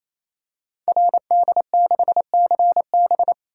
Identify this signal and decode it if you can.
Morse code, heard as a keyed tone, about 27 wpm, 700 Hz. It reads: RD6CB
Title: Code Groups - 5 Letter:Numbers - 28wpm